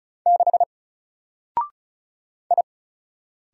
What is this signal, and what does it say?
Morse code, heard as a keyed tone, about 35 wpm, 700 Hz. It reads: 6 I
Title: Single Character (Letter:Number:Punctuation:Prosign) 35wpm